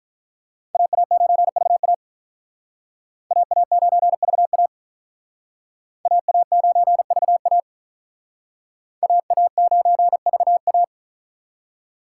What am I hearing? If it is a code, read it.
Morse code, heard as a keyed tone, about 44 wpm, 700 Hz. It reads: AA9VA AA9VA AA9VA AA9VA